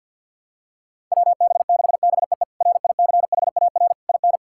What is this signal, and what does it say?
Morse code, heard as a keyed tone, about 53 wpm, 700 Hz. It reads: WB6BEE RICHAR IN
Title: K1USN SST QSO Practice - 50wpm